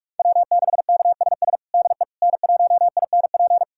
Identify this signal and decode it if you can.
Morse code, heard as a keyed tone, about 43 wpm, 700 Hz. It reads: W6KSS DE N1INP
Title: USA Callsign DE Callsign - 5 Characters - 45wpm